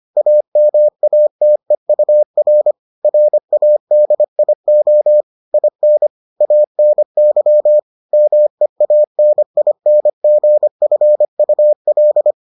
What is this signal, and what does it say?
Morse code, heard as a keyed tone, about 25 wpm, 600 Hz. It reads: AMATEUR RADIO IN ANY MEANINGFUL